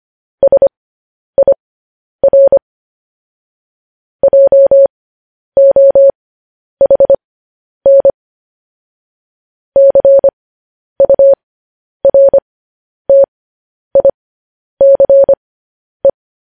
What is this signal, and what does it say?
Morse code, heard as a keyed tone, about 25 wpm, spaced as for 10 wpm, 550 Hz. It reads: SIR JOHN CURTICE
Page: news-headlines-in-morse-code-at-10-wpm